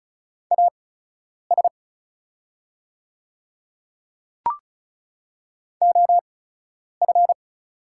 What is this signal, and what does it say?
Morse code, heard as a keyed tone, about 36 wpm, spaced as for 10 wpm, 700 Hz. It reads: AS OF
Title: 8x Character Spacing - Top 100 Words - 35wpm